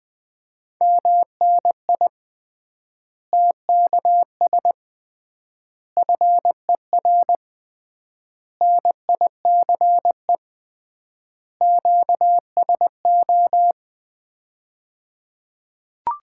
Morse code, heard as a keyed tone, about 20 wpm, 700 Hz. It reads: MNI TKS FER NICE QSO
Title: Common QSO Phrases - 3x Word-Spacing - 20wpm